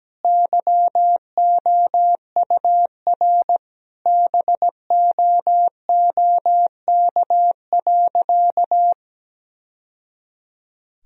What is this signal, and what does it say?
Morse code, heard as a keyed tone, about 17 wpm, 700 Hz. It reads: YOUR BOOK.
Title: Sentences from Top 1000 Words - Review Part 1 - 17wpm